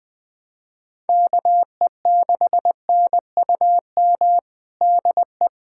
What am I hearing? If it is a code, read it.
Morse code, heard as a keyed tone, about 21 wpm, 700 Hz. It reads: KE6NUM DE
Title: USA Callsign DE Callsign - 6 Characters - 20wpm